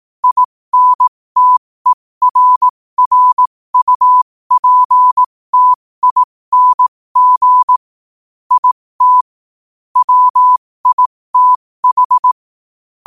Morse code, heard as a keyed tone, about 18 wpm, spaced as for 15 wpm, 1000 Hz. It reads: INTERRUPTING IT WITH
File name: qotdfiles / 15wpm